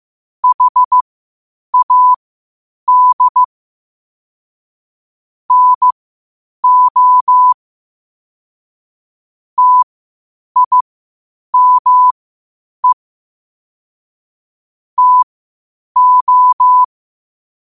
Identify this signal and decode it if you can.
Morse code, heard as a keyed tone, about 15 wpm, spaced as for 8 wpm, 1000 Hz. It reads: HAD NO TIME TO
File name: qotdfiles / 8wpm